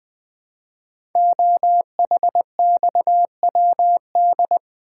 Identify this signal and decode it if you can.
Morse code, heard as a keyed tone, about 20 wpm, 700 Hz. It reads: OHXWD
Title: Code Groups - 5 Letters - 20wpm